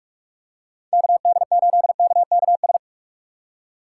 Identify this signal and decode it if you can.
Morse code, heard as a keyed tone, about 46 wpm, 700 Hz. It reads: KD8KKS
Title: USA Call Signs - 6 Characters - 3x Character Spacing - 45wpm